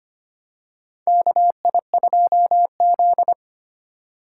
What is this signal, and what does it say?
Morse code, heard as a keyed tone, about 25 wpm, 700 Hz. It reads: KI2Z
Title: USA Call Signs - 4 Characters - 3x Character Spacing - 25wpm